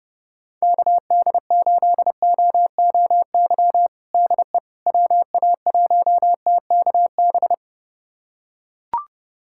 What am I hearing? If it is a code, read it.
Morse code, heard as a keyed tone, about 31 wpm, 700 Hz. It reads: KD8OOY DE WA1TKB
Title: USA Callsign DE Callsign - 6 Characters - 30wpm